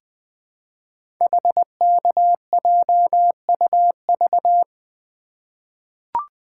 Morse code, heard as a keyed tone, about 21 wpm, 700 Hz. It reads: HKJUV
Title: Code Groups - 5 Letters - 20wpm